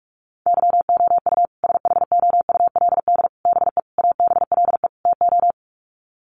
Morse code, heard as a keyed tone, about 44 wpm, 700 Hz. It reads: YOU SHOULD BE ABLE TO
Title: Sets of 5 Words Encore 45wpm